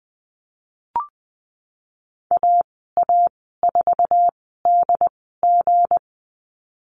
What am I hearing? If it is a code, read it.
Morse code, heard as a keyed tone, about 20 wpm, spaced as for 15 wpm, 700 Hz. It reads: AA4DG
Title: 186 Lesson 30g - Letter b - Call Signs 20wpm